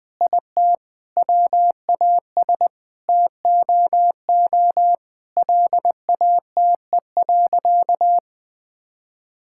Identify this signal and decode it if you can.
Morse code, heard as a keyed tone, about 20 wpm, 700 Hz. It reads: IT WAS TOO LATE.
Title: Sentences from Top 1000 Words - Review Part 6 - 20wpm